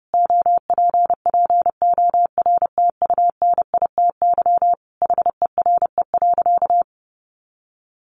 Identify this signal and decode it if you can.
Morse code, heard as a keyed tone, about 30 wpm, 700 Hz. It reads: OPPORTUNITY HERE.
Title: Sentences from Top 1300 Words 30wpm